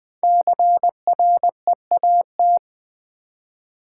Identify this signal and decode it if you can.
Morse code, heard as a keyed tone, about 20 wpm, 700 Hz. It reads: CREAT
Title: Build A Word - 5 Letters At A Time - 10-Letter Words - 20wpm